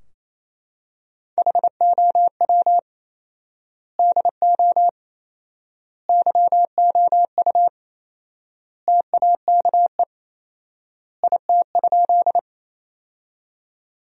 Morse code, heard as a keyed tone, about 28 wpm, 700 Hz. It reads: HOW DO YOU TAKE IT?
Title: 4x Word-Spacing - Sentences from Top 200 Words - 28wpm